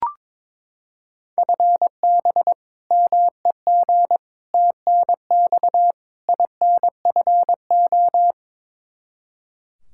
Morse code, heard as a keyed tone, about 22 wpm, 700 Hz. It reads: FB MEG TNX INFO
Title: Practice QSO - Exchange 3 - More Info - 22wpm